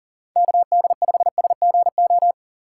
Morse code, heard as a keyed tone, about 40 wpm, 700 Hz. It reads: KD5SGO
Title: USA Call Signs - 6 Characters - 2x Character Spacing - 40wpm